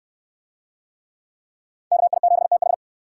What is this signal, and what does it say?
Morse code, heard as a keyed tone, about 65 wpm, 700 Hz. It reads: KI7TH